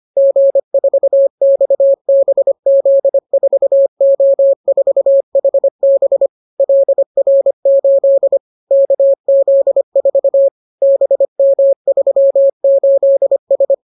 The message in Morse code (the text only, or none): G4XBZ4O4HB LR8 KZ4 BM38S